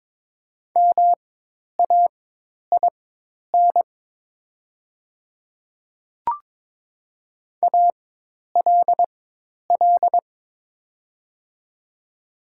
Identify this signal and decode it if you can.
Morse code, heard as a keyed tone, about 22 wpm, spaced as for 10 wpm, 700 Hz. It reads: MAIN ALL
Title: 4x Character Spacing - Top 500 Words - 22wpm